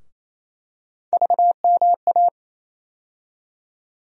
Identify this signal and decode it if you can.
Morse code, heard as a keyed tone, about 27 wpm, 700 Hz. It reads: VMA
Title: USA Call Sign Postfixes - 3 Letters 28wpm